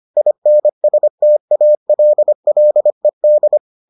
Morse code, heard as a keyed tone, about 25 wpm, 600 Hz. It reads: INSTALLED